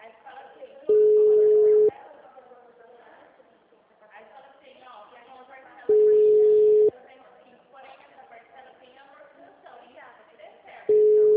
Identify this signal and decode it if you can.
Ringback tone (Europe, 425 Hz), with speech.